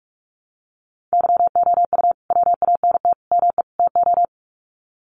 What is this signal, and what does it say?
Morse code, heard as a keyed tone, about 44 wpm, 700 Hz. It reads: YOU WANT ME TO
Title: Sets of 4 Words 45wpm